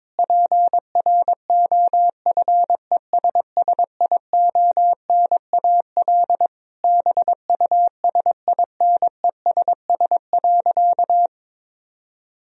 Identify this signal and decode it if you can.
Morse code, heard as a keyed tone, about 21 wpm, 700 Hz. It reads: PROFESSIONAL BUSINESS.
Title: Sentences from Top 1000 Words - Review Part 4 - 22wpm